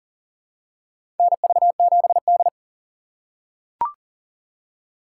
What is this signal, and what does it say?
Morse code, heard as a keyed tone, about 40 wpm, 700 Hz. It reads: NV7D
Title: USA Call Signs - 4 Characters - 4x Character Spacing - 40wpm